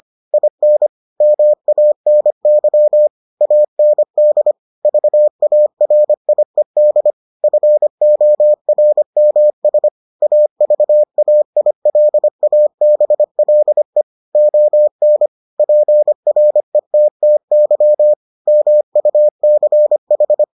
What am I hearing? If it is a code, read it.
Morse code, heard as a keyed tone, about 25 wpm, 600 Hz. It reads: IN MANY AND VARIED FORMS AVAILABLE ON PRETTY MUCH